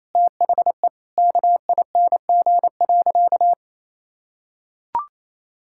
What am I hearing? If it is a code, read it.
Morse code, heard as a keyed tone, about 28 wpm, 700 Hz. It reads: THE KING.
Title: Sentences from Top 1600 Words 28wpm